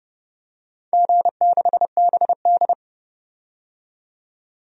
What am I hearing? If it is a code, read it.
Morse code, heard as a keyed tone, about 30 wpm, 700 Hz. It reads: G6BD
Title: DXCC-100 Call Signs - 4 Characters - 30wpm